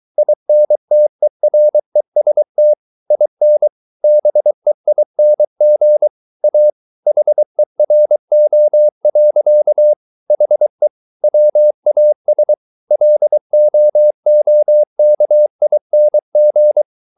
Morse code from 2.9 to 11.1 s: IN BEING A HERO. HE

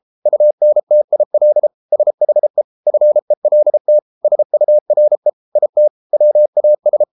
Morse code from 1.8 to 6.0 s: SHE FELT SURE IT

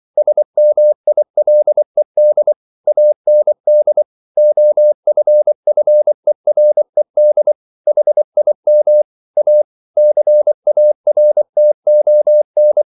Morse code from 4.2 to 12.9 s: OFFERED HIM A CARTON

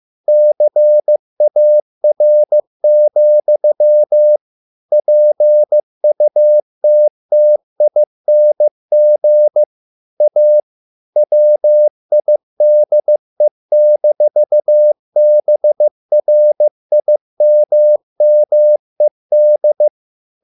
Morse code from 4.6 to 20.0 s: PUTTING A WIDE-BRIMMED

